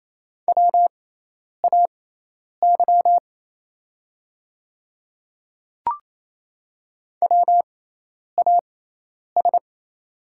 Morse code, heard as a keyed tone, about 28 wpm, spaced as for 10 wpm, 700 Hz. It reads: WAY WAS
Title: 6x Character Spacing - Top 100 Words - 28wpm